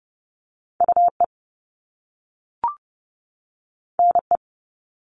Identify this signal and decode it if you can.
Morse code, heard as a keyed tone, about 30 wpm, 700 Hz. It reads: UE NE
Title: Top Two Letter Combinations in Words 30wpm